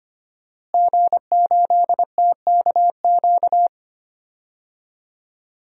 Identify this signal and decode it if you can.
Morse code, heard as a keyed tone, about 25 wpm, 700 Hz. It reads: G8TKQ
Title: DXCC-100 Call Signs - 5 Characters - Repeated 2x - 25wpm